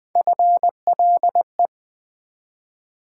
Morse code, heard as a keyed tone, about 20 wpm, 700 Hz. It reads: FLE
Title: Build A Word - 1 Letter At A Time - 5-Letter Words - 20wpm